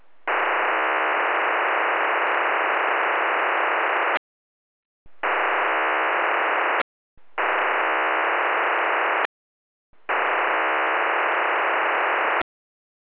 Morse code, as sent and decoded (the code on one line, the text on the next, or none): none